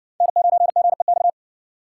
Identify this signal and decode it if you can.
Morse code, heard as a keyed tone, about 60 wpm, 700 Hz. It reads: N9GEX